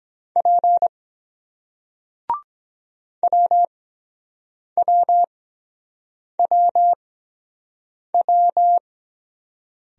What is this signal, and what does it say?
Morse code, heard as a keyed tone, about 23 wpm, spaced as for 7 wpm, 700 Hz. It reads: P WWWW